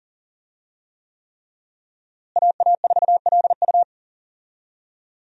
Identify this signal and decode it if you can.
Morse code, heard as a keyed tone, about 37 wpm, 700 Hz. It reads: AA4LU
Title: USA Call Signs - 5 Characters - Repeated 3x - 40wpm